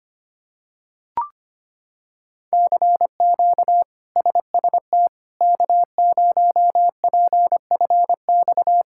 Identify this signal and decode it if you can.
Morse code, heard as a keyed tone, about 25 wpm, 700 Hz. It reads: CQ SST K0PFX
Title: K1USN SST QSO Practice - 25wpm